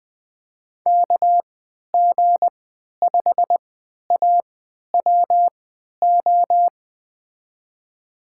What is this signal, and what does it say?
Morse code, heard as a keyed tone, about 20 wpm, spaced as for 11 wpm, 700 Hz. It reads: KG5AWO
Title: USA Call Signs - 6 Characters - 3x Character Spacing - 20wpm